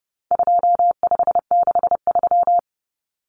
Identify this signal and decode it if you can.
Morse code, heard as a keyed tone, about 30 wpm, 700 Hz. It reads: 2563
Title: Common Cut Numbers - 4-Digit 30wpm